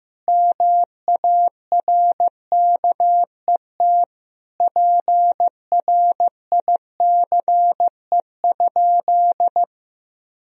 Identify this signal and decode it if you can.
Morse code, heard as a keyed tone, about 15 wpm, 700 Hz. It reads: MARKET PRICE?
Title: Sentences from Top 1000 Words - Review Part 3 - 15wpm